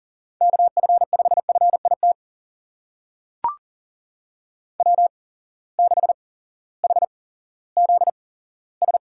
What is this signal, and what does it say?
Morse code, heard as a keyed tone, about 37 wpm, spaced as for 18 wpm, 700 Hz. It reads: KF5FIT W6HZS
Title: USA Call Signs - 8x Character Spacing - 40wpm